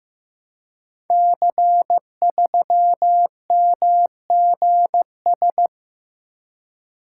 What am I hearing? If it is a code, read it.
Morse code, heard as a keyed tone, about 15 wpm, 700 Hz. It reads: C3MGS
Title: Code Groups - 5 Letter:Numbers - 15wpm